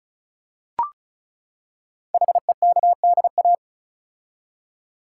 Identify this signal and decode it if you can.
Morse code, heard as a keyed tone, about 36 wpm, 690 Hz. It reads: HEKDA
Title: Code Groups - 5 Letters - 35wpm